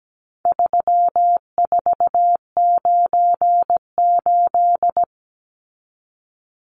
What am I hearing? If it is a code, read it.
Morse code, heard as a keyed tone, about 17 wpm, 700 Hz. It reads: 3498
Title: Numbers - 4-Digit 17wpm